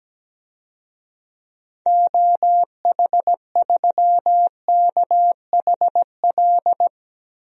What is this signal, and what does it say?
Morse code, heard as a keyed tone, about 17 wpm, 700 Hz. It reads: OH3KHL
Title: DXCC-100 Call Signs - 6 Characters - Repeated 3x - 17wpm